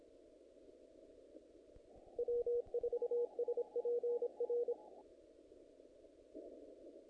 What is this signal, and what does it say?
Morse code, heard as a keyed tone, about 27 wpm, 480 Hz. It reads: W4SPR
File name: contests / naqp-cw-k5tr-op-k5pi- -1800